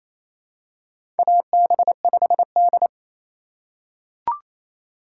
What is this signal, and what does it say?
Morse code, heard as a keyed tone, about 27 wpm, 700 Hz. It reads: AB5D
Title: USA Call Signs - 3x Character Spacing - 28wpm